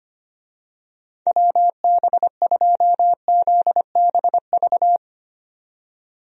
Morse code, heard as a keyed tone, about 25 wpm, 700 Hz. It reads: WB2ZBV
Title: USA Call Signs - 6 Characters - 25wpm